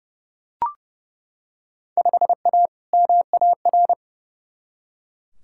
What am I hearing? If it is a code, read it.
Morse code, heard as a keyed tone, about 32 wpm, 700 Hz. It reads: 5A MAR
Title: Field Day Exchanges Encore - 30wpm